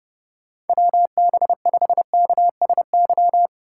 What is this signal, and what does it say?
Morse code, heard as a keyed tone, about 32 wpm, 700 Hz. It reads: WB5KSY
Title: USA Call Signs - 6 Characters - 3x Character Spacing - 30wpm